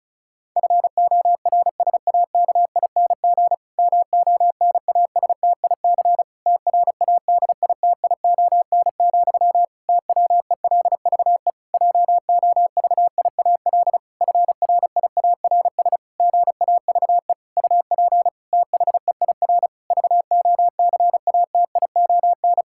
Morse code, heard as a keyed tone, about 36 wpm, 700 Hz. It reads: FORSAKING MONASTIC TRADITION, TWELVE JOVIAL FRIARS GAVE UP THEIR VOCATION
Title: Pangram Challenge v2 - 35wpm